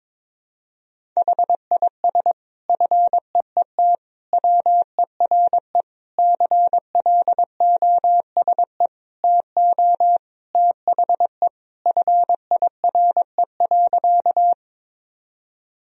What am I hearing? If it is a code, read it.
Morse code, heard as a keyed tone, about 22 wpm, 700 Hz. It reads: HIS FEET WERE CLOSE TO THE FIRE.
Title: Sentences from Top 1000 Words - Review Part 1 - 22wpm